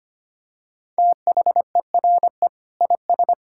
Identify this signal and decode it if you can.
Morse code, heard as a keyed tone, about 25 wpm, 700 Hz. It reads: THERE IS
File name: practice / 25wpm / Sentences with 3-Word Phrases - 25wpm